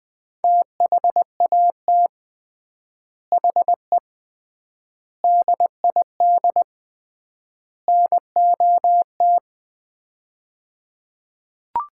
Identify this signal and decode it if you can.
Morse code, heard as a keyed tone, about 20 wpm, 700 Hz. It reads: THAT HE DID NOT
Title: 3x Word-Spacing - Sets of 4 Words - 20wpm